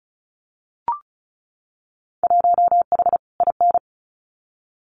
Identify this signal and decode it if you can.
Morse code, heard as a keyed tone, about 34 wpm, 700 Hz. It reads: 1H IN
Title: Winter Field Day Exchanges 35wpm